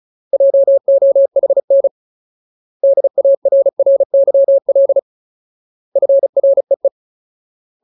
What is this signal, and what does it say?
Morse code, heard as a keyed tone, about 34 wpm, 550 Hz. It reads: JOHN DARRYL FREE
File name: Ultimate Word List / 35WPM Ultimate